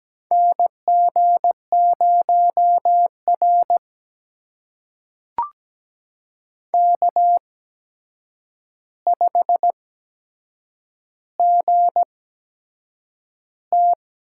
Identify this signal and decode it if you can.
Morse code, heard as a keyed tone, about 17 wpm, spaced as for 7 wpm, 700 Hz. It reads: NG0R K5GT
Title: USA Call Signs - 4 Characters - 8x Character Spacing - 17wpm